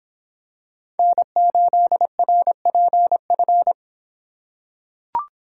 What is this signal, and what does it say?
Morse code, heard as a keyed tone, about 25 wpm, 700 Hz. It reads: N8RPF